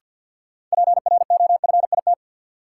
Morse code, heard as a keyed tone, about 50 wpm, 700 Hz. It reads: PROFIT
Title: Sentences from Top 800 Words - New Word Drill - 50wpm